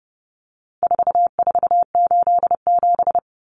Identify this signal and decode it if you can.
Morse code, heard as a keyed tone, about 28 wpm, 700 Hz. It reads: 4487
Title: Numbers - 4-Digit 30wpm